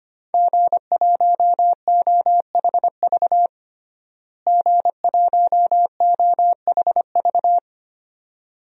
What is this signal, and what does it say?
Morse code, heard as a keyed tone, about 25 wpm, 700 Hz. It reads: G1OHV G1OHV
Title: DXCC-100 Call Signs - 5 Characters - Repeated 2x - 25wpm